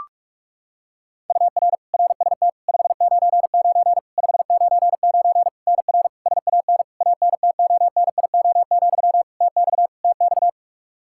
Morse code, heard as a keyed tone, about 41 wpm, 700 Hz. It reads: UR RST 599 599 NR SAN ANTONIO, TX TX
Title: K1USN SST QSO Practice 45wpm